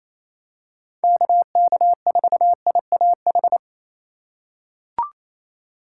Morse code, heard as a keyed tone, about 28 wpm, 700 Hz. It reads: KK4IAH
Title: USA Call Signs - 6 Characters - 4x Character Spacing - 28wpm